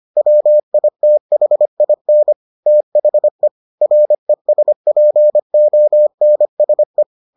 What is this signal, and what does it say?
Morse code, heard as a keyed tone, about 25 wpm, 600 Hz. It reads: WITHIN THE RESPONSE